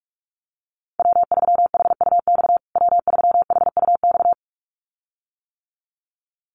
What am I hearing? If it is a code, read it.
Morse code, heard as a keyed tone, about 44 wpm, 700 Hz. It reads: W3HUX W3HUX
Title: USA Call Signs Repeated 2x 45wpm